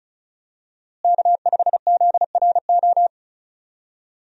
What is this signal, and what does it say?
Morse code, heard as a keyed tone, about 36 wpm, 700 Hz. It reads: K5ZRO
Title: USA Call Signs - 5 Characters - Repeated 3x - 35wpm